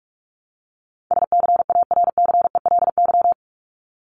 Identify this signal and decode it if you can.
Morse code, heard as a keyed tone, about 44 wpm, 700 Hz. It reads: SCARCELY
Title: LY Suffix - Top 100 Words with Base Words 45wpm